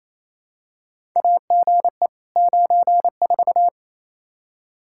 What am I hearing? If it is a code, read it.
Morse code, heard as a keyed tone, about 27 wpm, 700 Hz. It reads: AGE 94
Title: Practice QSO - Exchange 3 - More Info - 28wpm